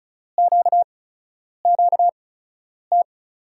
Morse code, heard as a keyed tone, about 36 wpm, spaced as for 10 wpm, 700 Hz. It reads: QQT
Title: USA Call Signs - 5 Characters - 8x Character Spacing - 35wpm